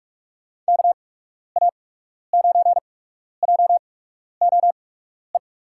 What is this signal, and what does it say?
Morse code, heard as a keyed tone, about 47 wpm, spaced as for 12 wpm, 690 Hz. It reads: KA9JOE
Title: USA Call Signs - 8x Character Spacing - 45wpm